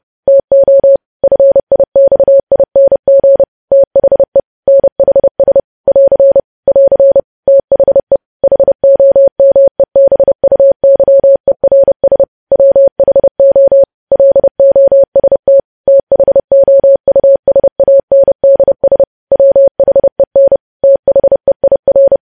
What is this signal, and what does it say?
Morse code, heard as a keyed tone, about 30 wpm, 550 Hz. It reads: TO FIXING THE NHS <AR> <AR> THE HOMEBUYERS WHO LOST THOUSANDS WHEN THEIR
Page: news headlines in morse code at 30 wpm